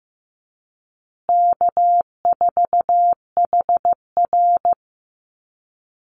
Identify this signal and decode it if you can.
Morse code, heard as a keyed tone, about 15 wpm, 700 Hz. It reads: K4HR
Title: Call Signs Repeated 2x - 15wpm